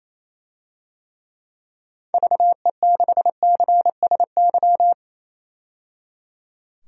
Morse code, heard as a keyed tone, about 27 wpm, 700 Hz. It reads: VE6CSY
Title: DXCC-100 Call Signs - 6 Characters - 28wpm